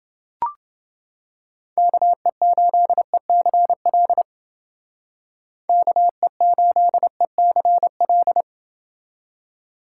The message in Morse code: KE8ECL KE8ECL

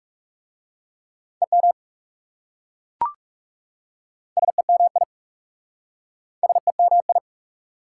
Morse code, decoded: EM SEMI SEMI

